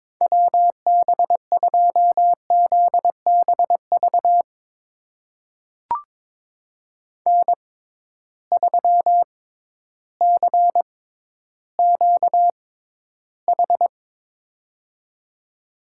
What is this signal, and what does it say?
Morse code, heard as a keyed tone, about 22 wpm, spaced as for 12 wpm, 700 Hz. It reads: WB2ZBV N3CQH
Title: USA Call Signs - 6x Character Spacing - 22wpm